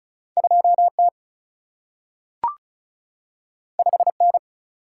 Morse code, heard as a keyed tone, about 36 wpm, 700 Hz. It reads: 2T 5N
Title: Single-Letter Single-Number 35wpm